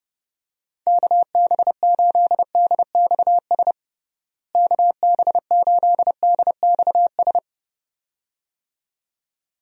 Morse code, heard as a keyed tone, about 30 wpm, 700 Hz. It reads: KB8DXS KB8DXS
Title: USA Call Signs - 6 Characters - Repeated 2x - 30wpm